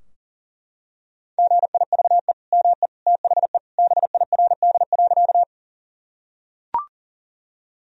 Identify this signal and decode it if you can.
Morse code, heard as a keyed tone, about 40 wpm, 700 Hz. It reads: GIVE ME THE BIRD.
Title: Sentences from Top 1000 Words - Review Part 2 - 40wpm